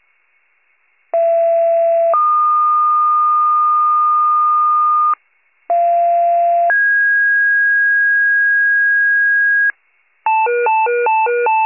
Two-tone page, heard as a one-second tone, then a three-second tone, about 670 then 1220 Hz, about 690 then 1650 Hz, with noise. It is heard from 1.1 to 9.7 s.
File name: wcalert